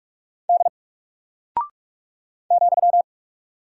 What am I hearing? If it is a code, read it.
Morse code, heard as a keyed tone, about 48 wpm, 700 Hz. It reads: D ,
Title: Single Character (Letter:Number:Punctuation:Prosign) Alt 45wpm